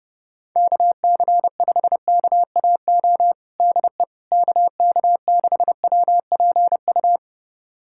Morse code, heard as a keyed tone, about 30 wpm, 690 Hz. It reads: KC5KAO DE KK6WPU
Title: USA Callsign DE Callsign 30wpm